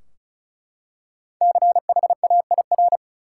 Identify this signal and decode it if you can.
Morse code, heard as a keyed tone, about 34 wpm, 700 Hz. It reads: CHAIR
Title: Five Letter Words - 35wpm